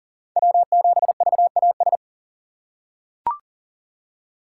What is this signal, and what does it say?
Morse code, heard as a keyed tone, about 37 wpm, 700 Hz. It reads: W7VAS
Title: K1USN SST QSO Practice - 40wpm